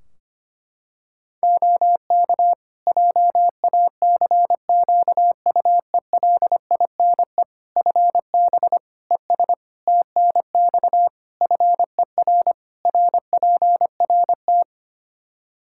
Morse code, heard as a keyed tone, about 25 wpm, 700 Hz. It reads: OK JACQUELINE FB ES TNX FER RPRT
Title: K1USN SST QSO Practice 25wpm